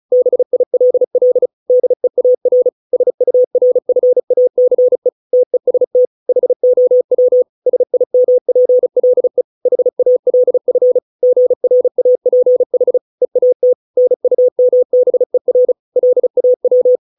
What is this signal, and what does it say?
Morse code, heard as a keyed tone, about 34 wpm, 500 Hz. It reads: BILL DEAR SURFACE TEST HOW SIMPLE HALF GRAPH EAT NUMBER LAW